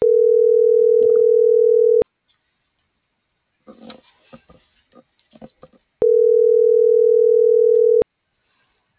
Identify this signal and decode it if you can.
Ringback tone (North America), 440 + 480 Hz.